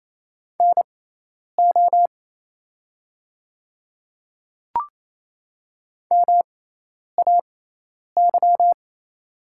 Morse code, heard as a keyed tone, about 28 wpm, spaced as for 10 wpm, 700 Hz. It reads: NO MAY
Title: 6x Character Spacing - Top 100 Words - 28wpm